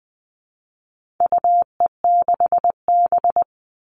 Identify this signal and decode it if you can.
Morse code, heard as a keyed tone, about 20 wpm, 700 Hz. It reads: UE6B
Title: All Cut Numbers - 4-Digit 20wpm